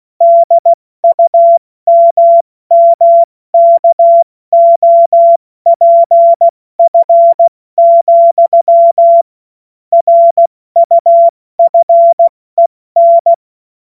Morse code, heard as a keyed tone, about 16 wpm, 680 Hz. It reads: DUMMKOPF, RUFEN
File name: Morsekurs / Witze / Witz 04. Januar 2022, 70 F80